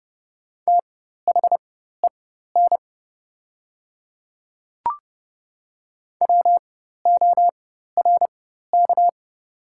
Morse code, heard as a keyed tone, about 30 wpm, spaced as for 14 wpm, 700 Hz. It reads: THEN WORK
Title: 4x Character Spacing - Top 200 Words - 30wpm